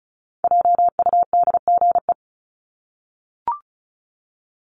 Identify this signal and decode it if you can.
Morse code, heard as a keyed tone, about 34 wpm, 700 Hz. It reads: JUDGE
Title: Top 1751-2000 Words 35wpm